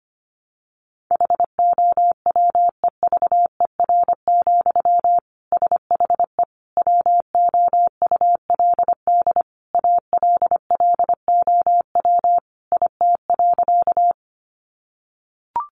Morse code, heard as a keyed tone, about 25 wpm, 700 Hz. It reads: HOWEVER, SHE WOULD ALLOW IT.
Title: Sentences from Top 700 Words 25wpm